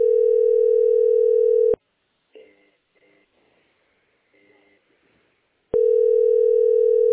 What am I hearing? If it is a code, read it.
Ringback tone (North America), 440 + 480 Hz.